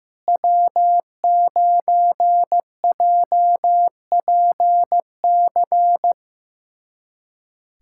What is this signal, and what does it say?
Morse code, heard as a keyed tone, about 15 wpm, 700 Hz. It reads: W9JPC
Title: USA Callsign DE Callsign - 5 Characters - 15wpm